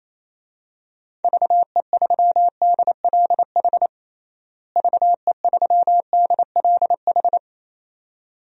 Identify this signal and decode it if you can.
Morse code, heard as a keyed tone, about 28 wpm, 700 Hz. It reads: VE3DLH VE3DLH
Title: DXCC-100 Call Signs - 6 Characters - Repeated 2x - 28wpm